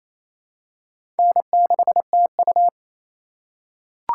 Morse code, heard as a keyed tone, about 28 wpm, 700 Hz. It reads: N6TU
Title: USA Call Signs - 2x Character Spacing - 28wpm